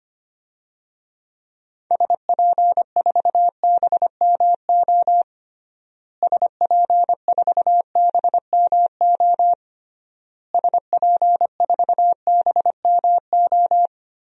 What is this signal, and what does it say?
Morse code, heard as a keyed tone, about 25 wpm, 700 Hz. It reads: SP4BMO SP4BMO SP4BMO
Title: DXCC-100 Call Signs - 6 Characters - Repeated 3x - 25wpm